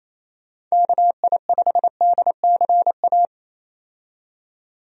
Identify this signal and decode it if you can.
Morse code, heard as a keyed tone, about 27 wpm, 700 Hz. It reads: KI5DCA
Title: USA Call Signs - 6 Characters - 28wpm